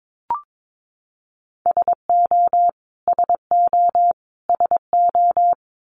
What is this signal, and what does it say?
Morse code, heard as a keyed tone, about 22 wpm, 700 Hz. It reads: SO SO SO
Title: Top 200 Words Repeated 22wpm